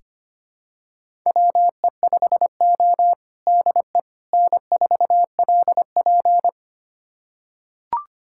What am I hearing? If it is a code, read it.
Morse code, heard as a keyed tone, about 25 wpm, 700 Hz. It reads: WE5O DE N4LP
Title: USA Callsign DE Callsign - 4 Characters - 25wpm